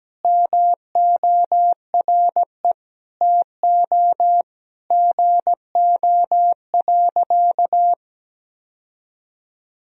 Morse code, heard as a keyed tone, about 17 wpm, 700 Hz. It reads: MORE TO GO.
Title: Sentences from Top 1000 Words - Review Part 2 - 17wpm